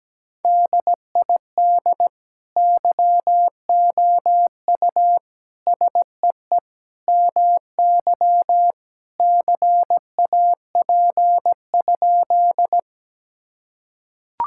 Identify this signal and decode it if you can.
Morse code, heard as a keyed tone, about 17 wpm, 700 Hz. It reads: DID YOU SEE MY CAP?
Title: Sentences from Top 1600 Words 17wpm